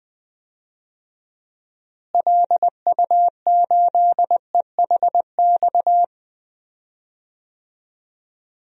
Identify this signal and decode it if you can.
Morse code, heard as a keyed tone, about 21 wpm, 700 Hz. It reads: LU8EHX
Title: DXCC-100 Call Signs - 6 Characters - Repeated 3x - 20wpm